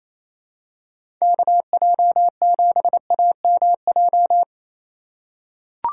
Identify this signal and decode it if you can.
Morse code, heard as a keyed tone, about 28 wpm, 700 Hz. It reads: KJ7AMJ
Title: USA Call Signs - 6 Characters - 4x Character Spacing - 28wpm